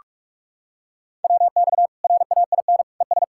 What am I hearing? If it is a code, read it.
Morse code, heard as a keyed tone, about 46 wpm, 700 Hz. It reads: WX RAIN ES
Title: Practice QSO - Exchange 2 - Info - 45wpm